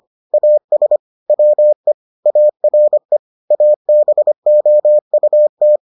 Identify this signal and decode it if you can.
Morse code, heard as a keyed tone, about 25 wpm, 600 Hz. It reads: AS WE ARE ABOUT